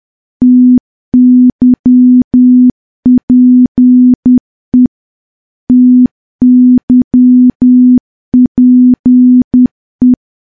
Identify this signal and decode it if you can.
Morse code, heard as a keyed tone, about 10 wpm, 250 Hz. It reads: TYPE TYPE
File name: morse / morse-training-10WPM